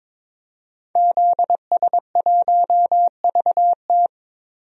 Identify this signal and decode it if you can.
Morse code, heard as a keyed tone, about 22 wpm, 700 Hz. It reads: ZS1VT
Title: DXCC-100 Call Signs - 5 Characters - 22wpm